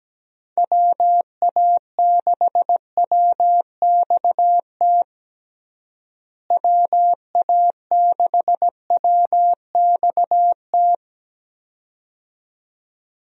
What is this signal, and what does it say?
Morse code, heard as a keyed tone, about 17 wpm, 700 Hz. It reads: WA6WXT WA6WXT
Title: USA Call Signs - 6 Characters - Repeated 2x - 17wpm